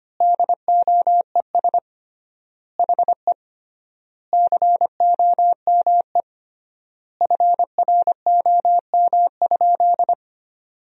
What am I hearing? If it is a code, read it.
Morse code, heard as a keyed tone, about 25 wpm, 700 Hz. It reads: DOES HE COME FROM?
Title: 3x Word-Spacing - Sentences from Top 300 Words - 25wpm